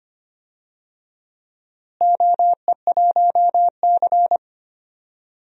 Morse code, heard as a keyed tone, about 25 wpm, 700 Hz. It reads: OE1C
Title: DXCC-100 Call Signs - 4 Characters - Repeated 2x - 25wpm